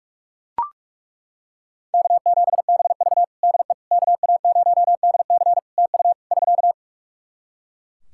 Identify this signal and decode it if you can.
Morse code, heard as a keyed tone, about 41 wpm, 700 Hz. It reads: K7BV DE KA0DC TU <SK>
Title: Practice QSO - Exchange 4 - Ending - 45wpm